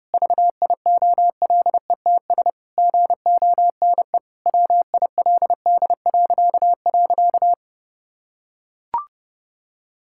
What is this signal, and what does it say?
Morse code, heard as a keyed tone, about 30 wpm, 700 Hz. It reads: VIOLETS GONE WILD..
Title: Pangram Challenge v2 - 30wpm